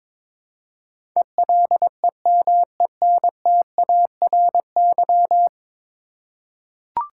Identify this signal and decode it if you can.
Morse code, heard as a keyed tone, about 22 wpm, 700 Hz. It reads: ELEMENTARY
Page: morse-code-ninja-practice-22wpm